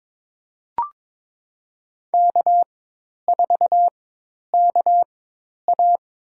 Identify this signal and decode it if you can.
Morse code, heard as a keyed tone, about 21 wpm, spaced as for 10 wpm, 700 Hz. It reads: K4KA
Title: USA Call Signs - 4 Characters - 4x Character Spacing - 22wpm